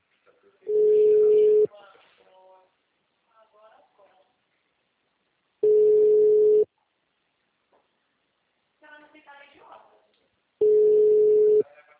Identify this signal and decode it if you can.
Ringback tone (Europe, 425 Hz), with speech.
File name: out- -2009- - - .78733